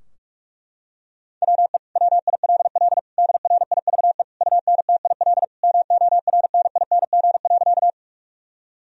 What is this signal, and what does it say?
Morse code, heard as a keyed tone, about 41 wpm, 700 Hz. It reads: WE WILL DRIVE UNTIL MORNING.
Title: Sentences from Top 1000 Words - Review Part 2 - 45wpm